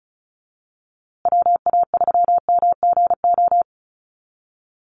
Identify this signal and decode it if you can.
Morse code, heard as a keyed tone, about 34 wpm, 700 Hz. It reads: WA3MGO
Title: USA Call Signs Repeated 3x 35wpm